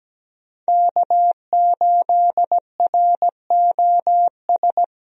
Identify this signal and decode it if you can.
Morse code, heard as a keyed tone, about 17 wpm, 700 Hz. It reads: K8ROS
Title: USA Call Signs - 5 Characters - Repeated 3x - 17wpm